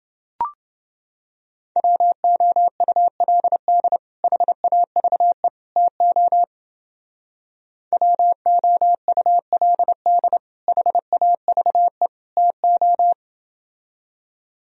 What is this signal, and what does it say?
Morse code, heard as a keyed tone, about 31 wpm, 700 Hz. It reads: WOULD HAVE TO WOULD HAVE TO